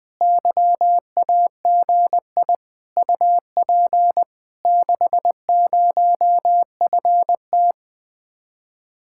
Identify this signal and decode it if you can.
Morse code, heard as a keyed tone, about 20 wpm, 700 Hz. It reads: YAGI UP 60FT
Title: Speed-Racing Common QSO Phrases Encore - 17wpm